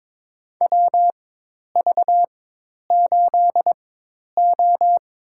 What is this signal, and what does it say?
Morse code, heard as a keyed tone, about 22 wpm, spaced as for 10 wpm, 700 Hz. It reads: WV8O
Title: USA Call Signs - 4 Characters - 4x Character Spacing - 22wpm